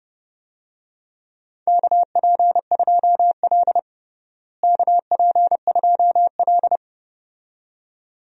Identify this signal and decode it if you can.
Morse code, heard as a keyed tone, about 30 wpm, 700 Hz. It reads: KP2L KP2L
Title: DXCC-100 Call Signs - 4 Characters - Repeated 2x - 30wpm